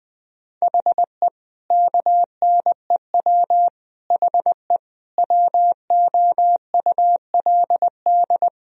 Morse code, heard as a keyed tone, about 20 wpm, 700 Hz. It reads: HE KNEW HE WOULD
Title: Sentences from Top 1000 Words - Review Part 2 - 20wpm